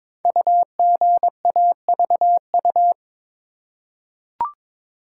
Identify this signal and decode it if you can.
Morse code, heard as a keyed tone, about 22 wpm, 700 Hz. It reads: UGAVU
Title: Code Groups - 5 Letter:Numbers - 22wpm